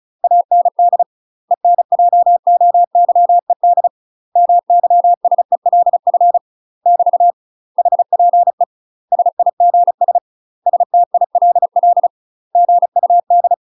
Morse code, heard as a keyed tone, about 34 wpm, 700 Hz. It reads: AND ENJOYED MYSELF <BT> HPE SIGS STILL GUD
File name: UK MASTER QSOs TRAINING / 35WPM / UK MASTER QSOs 0000